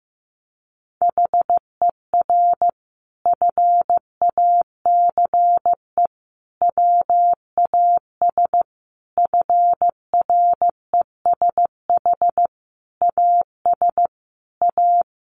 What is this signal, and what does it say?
Morse code, heard as a keyed tone, about 15 wpm, 700 Hz. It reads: HER FACE WAS FRESH AS A
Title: Sentences from Top 1200 Words 15wpm